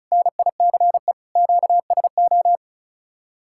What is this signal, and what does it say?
Morse code, heard as a keyed tone, about 43 wpm, 700 Hz. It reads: NICE QSO
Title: Speed-Racing Common QSO Phrases - 35wpm